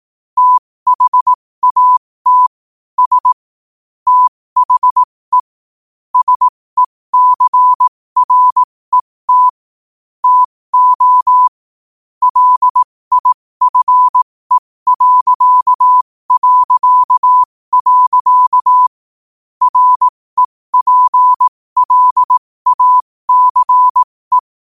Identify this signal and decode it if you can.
Morse code, heard as a keyed tone, about 18 wpm, spaced as for 15 wpm, 1000 Hz. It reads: THAT S THE SECRET TO LIFE... REPLACE
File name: qotdfiles / 15wpm